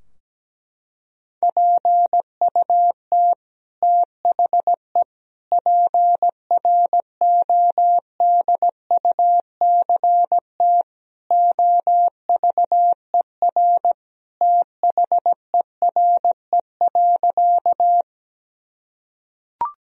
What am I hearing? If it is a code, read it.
Morse code, heard as a keyed tone, about 17 wpm, 700 Hz. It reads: PUT THE PRODUCT OVER THERE.
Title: Sentences from Top 1000 Words - Review Part 2 - 17wpm